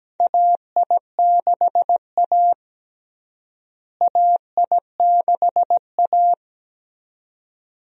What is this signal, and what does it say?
Morse code, heard as a keyed tone, about 17 wpm, 700 Hz. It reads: AI6A AI6A
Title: USA Call Signs - 4 Characters - Repeated 2x - 17wpm